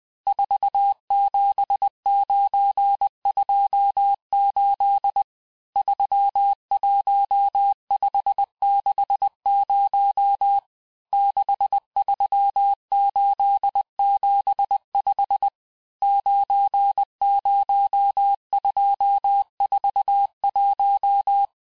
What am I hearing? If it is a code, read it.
Morse code, heard as a keyed tone, about 20 wpm, 800 Hz. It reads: 47928 31560 63875 90241
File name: cours-858-texte-chiffre-vitesse-19mpm